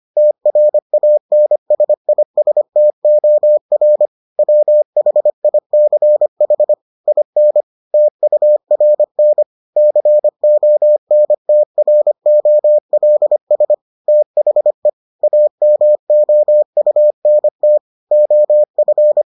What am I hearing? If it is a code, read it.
Morse code, heard as a keyed tone, about 25 wpm, 600 Hz. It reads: TRANSISTOR WHICH IN TURN CONTROLS THE AMOUNT OF